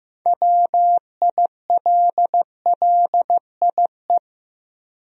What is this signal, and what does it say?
Morse code, heard as a keyed tone, about 15 wpm, 700 Hz. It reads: WILLIE
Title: USA Common Names - 15wpm